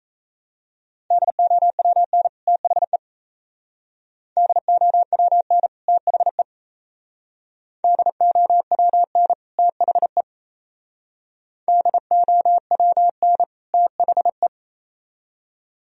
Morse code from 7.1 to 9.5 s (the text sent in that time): DOWN